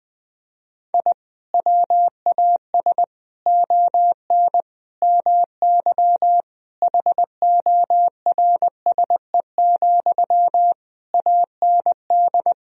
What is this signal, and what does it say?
Morse code, heard as a keyed tone, about 20 wpm, 700 Hz. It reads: I WAS ON MY HORSE, AND
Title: Sentences from Top 1000 Words - Review Part 3 - 20wpm